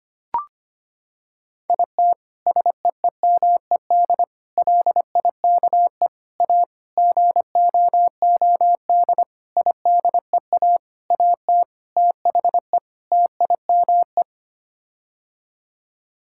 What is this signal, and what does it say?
Morse code, heard as a keyed tone, about 25 wpm, 700 Hz. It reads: IT SEEMED LIKE A GOOD IDEA AT THE TIME
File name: Sets of 9 Words - 25wpm